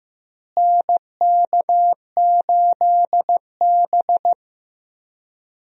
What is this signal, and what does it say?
Morse code, heard as a keyed tone, about 16 wpm, 700 Hz. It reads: NK8B
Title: USA Call Signs - 4 Characters - Repeated 2x - 15wpm